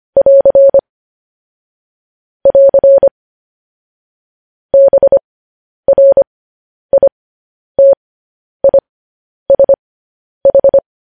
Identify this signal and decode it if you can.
Morse code, heard as a keyed tone, about 25 wpm, spaced as for 10 wpm, 550 Hz. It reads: <AR> <AR> BRITISH
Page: news-headlines-in-morse-code-at-10-wpm